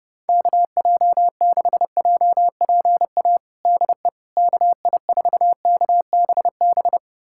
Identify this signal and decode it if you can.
Morse code, heard as a keyed tone, about 30 wpm, 700 Hz. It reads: KJ6JPA DE KI4KBB
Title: USA Callsign DE Callsign - 6 Characters - 30wpm